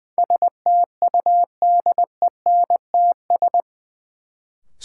Morse code, heard as a keyed tone, about 20 wpm, 700 Hz. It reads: STUDENTS
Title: Sets of 5 Words Encore - New Word Drill - 20wpm